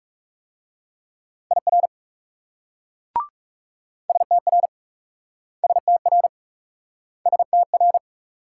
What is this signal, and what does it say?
Morse code, heard as a keyed tone, about 41 wpm, 700 Hz. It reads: IR STR STR STR